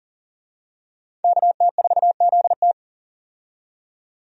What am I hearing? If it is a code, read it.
Morse code, heard as a keyed tone, about 40 wpm, 700 Hz. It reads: KT4ZT
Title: USA Call Signs - 5 Characters - 40wpm